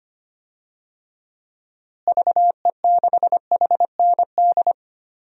Morse code, heard as a keyed tone, about 25 wpm, 700 Hz. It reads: VE6HND
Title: DXCC-100 Call Signs - 6 Characters - Repeated 2x - 25wpm